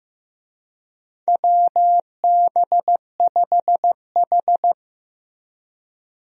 Morse code, heard as a keyed tone, about 15 wpm, 700 Hz. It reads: WB5H
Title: USA Call Signs - 4 Characters - Repeated 3x - 15wpm